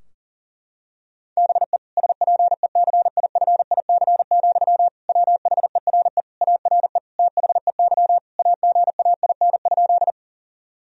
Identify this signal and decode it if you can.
Morse code, heard as a keyed tone, about 40 wpm, 700 Hz. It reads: BE SPECIFIC, WHERE ARE THEY AGAIN?
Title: Sentences from Top 1000 Words - Review Part 3 - 40wpm